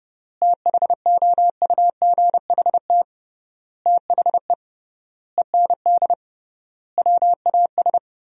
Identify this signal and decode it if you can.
Morse code, heard as a keyed tone, about 30 wpm, 700 Hz. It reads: THOUGHT THE END WAS
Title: 3x Word-Spacing - Sentences from Top 300 Words - 30wpm